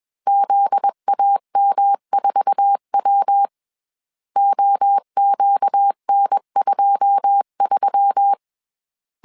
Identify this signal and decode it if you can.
Morse code, heard as a keyed tone, about 21 wpm, 800 Hz. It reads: ZAM4W OQN23
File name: CW / DET / 1 9 1500C S C 20wpm 2310